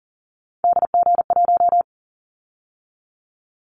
Morse code, heard as a keyed tone, about 40 wpm, 700 Hz. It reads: DG1
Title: Callsign Prefixes for Top 25 DXCC - 40wpm